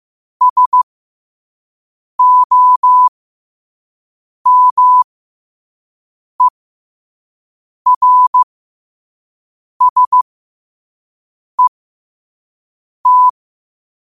Morse code, heard as a keyed tone, about 15 wpm, spaced as for 5 wpm, 1000 Hz. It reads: SOMERSET
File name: qotdfiles / 5wpm